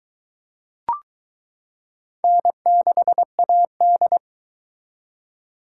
Morse code, heard as a keyed tone, about 22 wpm, 700 Hz. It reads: N6AD